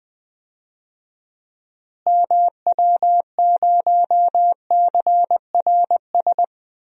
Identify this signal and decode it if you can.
Morse code, heard as a keyed tone, about 20 wpm, 700 Hz. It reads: MW0CRS
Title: DXCC-100 Call Signs - 6 Characters - Repeated 2x - 20wpm